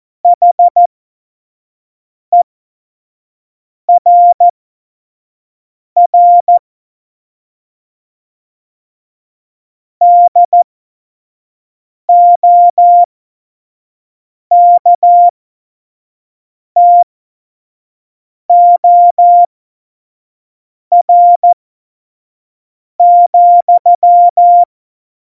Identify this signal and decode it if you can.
Morse code, heard as a keyed tone, about 14 wpm, spaced as for 5 wpm, 700 Hz. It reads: HERR DOKTOR,